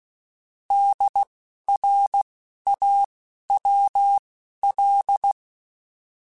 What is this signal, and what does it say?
Morse code, heard as a keyed tone, about 16 wpm, spaced as for 12 wpm, 800 Hz. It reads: DRAWL
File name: Alphabetical / Alphabetical mp3s / Ds @10